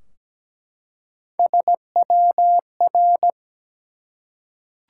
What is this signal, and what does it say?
Morse code, heard as a keyed tone, about 17 wpm, 700 Hz. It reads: SWR
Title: Top 180 QSO Elements 17wpm